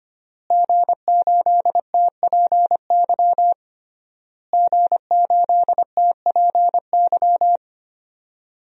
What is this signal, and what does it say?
Morse code, heard as a keyed tone, about 25 wpm, 700 Hz. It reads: G8TPY G8TPY
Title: DXCC-100 Call Signs - 5 Characters - Repeated 3x - 25wpm